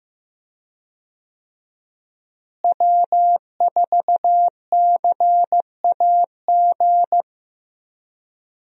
Morse code, heard as a keyed tone, about 15 wpm, 700 Hz. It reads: W4CAG
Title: USA Call Signs - 5 Characters - Repeated 2x - 15wpm